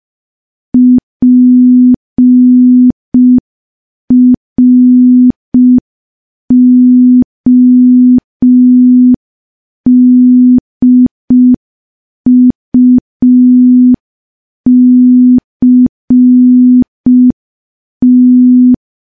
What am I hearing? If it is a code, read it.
Morse code, heard as a keyed tone, about 5 wpm, 250 Hz. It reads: PRODUCT
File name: morse / morse-training-05WPM